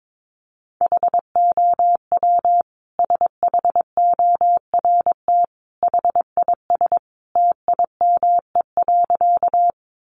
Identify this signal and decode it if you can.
Morse code, heard as a keyed tone, about 22 wpm, 700 Hz. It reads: HOW SHORT HIS TIME.
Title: Sentences from Top 400 Words 22wpm